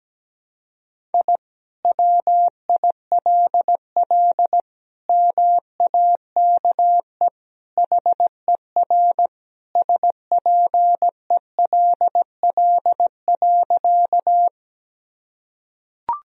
Morse code, heard as a keyed tone, about 17 wpm, 700 Hz. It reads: I WILL MAKE HER SPELL.
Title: Sentences from Top 1000 Words - Review Part 1 - 17wpm